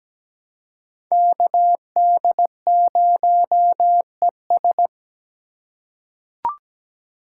Morse code, heard as a keyed tone, about 17 wpm, 700 Hz. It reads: KD0ES
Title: USA Call Signs - 5 Characters - 3x Character Spacing - 17wpm